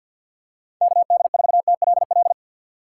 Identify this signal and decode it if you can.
Morse code, heard as a keyed tone, about 48 wpm, 700 Hz. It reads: KD4TLL
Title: USA Call Signs - 6 Characters - 3x Character Spacing - 50wpm